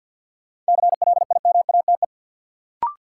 Morse code, heard as a keyed tone, about 47 wpm, 700 Hz. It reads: CLIMATE
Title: Sentences from Top 1600 Words - New Word Drill - 50wpm